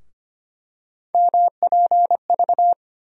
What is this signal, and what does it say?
Morse code, heard as a keyed tone, about 25 wpm, 700 Hz. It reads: MPV
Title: USA Call Sign Postfixes - 3 Letters 25wpm